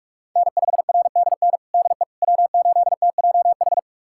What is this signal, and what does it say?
Morse code, heard as a keyed tone, about 46 wpm, 700 Hz. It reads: N5RDN DE W8TJH
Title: USA Callsign DE Callsign - 5 Characters - 45wpm